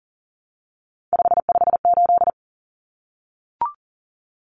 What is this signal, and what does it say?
Morse code, heard as a keyed tone, about 40 wpm, 700 Hz. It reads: 558
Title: Common Cut Numbers - 3-Digit 40wpm